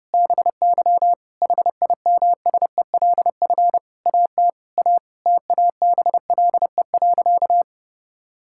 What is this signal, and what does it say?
Morse code, heard as a keyed tone, about 30 wpm, 700 Hz. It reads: BY HIMSELF AT A TABLE.
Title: Sentences from Top 1600 Words 30wpm